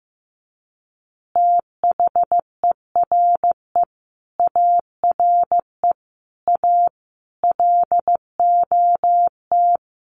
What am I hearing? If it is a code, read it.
Morse code, heard as a keyed tone, about 15 wpm, 700 Hz. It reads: THERE ARE A LOT
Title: Easy Phrases 15wpm